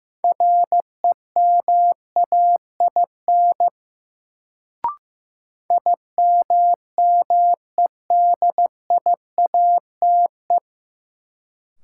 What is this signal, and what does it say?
Morse code, heard as a keyed tone, about 15 wpm, 700 Hz. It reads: REMAIN IMMEDIATE
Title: Sentences from Top 1200 Words - New Word Drill - 15wpm